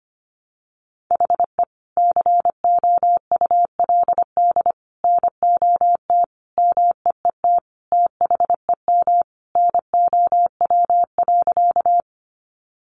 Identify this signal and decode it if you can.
Morse code, heard as a keyed tone, about 25 wpm, 700 Hz. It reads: HE COULD NOT MEET THEM NOW.
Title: Sentences from Top 1100 Words 25wpm